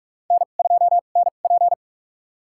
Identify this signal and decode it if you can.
Morse code, heard as a keyed tone, about 44 wpm, spaced as for 32 wpm, 690 Hz. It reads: N2NP
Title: USA Call Signs - 4 Characters - 2x Character Spacing - 45wpm